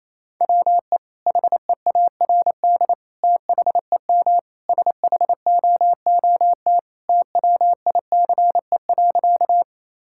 Morse code, heard as a keyed tone, about 28 wpm, 700 Hz. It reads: WE HEARD THEM SHOOT TWICE.
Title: Sentences from Top 1600 Words 28wpm